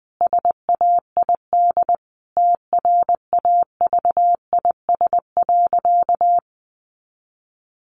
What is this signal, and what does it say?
Morse code, heard as a keyed tone, about 20 wpm, 700 Hz. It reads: SAID TRAVIS.
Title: A Sound Of Thunder 20wpm